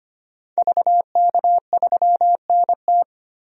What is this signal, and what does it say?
Morse code, heard as a keyed tone, about 25 wpm, 700 Hz. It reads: VK3NT
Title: DXCC-100 Call Signs - 5 Characters - 25wpm